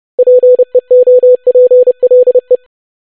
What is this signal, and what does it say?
Morse code, heard as a keyed tone, about 30 wpm, 500 Hz. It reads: PEOPLE
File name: basic100-25wpm